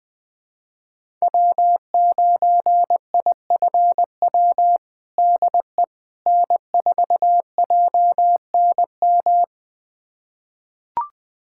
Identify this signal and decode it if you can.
Morse code, heard as a keyed tone, about 21 wpm, 700 Hz. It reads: W9IFW DE N4JNM
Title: USA Callsign DE Callsign - 5 Characters - 20wpm